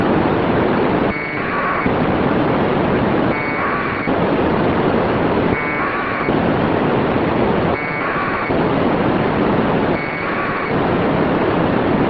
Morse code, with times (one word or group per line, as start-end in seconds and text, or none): none